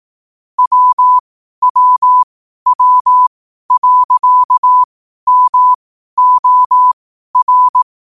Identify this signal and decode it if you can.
Morse code, heard as a keyed tone, about 18 wpm, spaced as for 12 wpm, 1000 Hz. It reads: WWW.MOR
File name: qotdfiles / 12wpm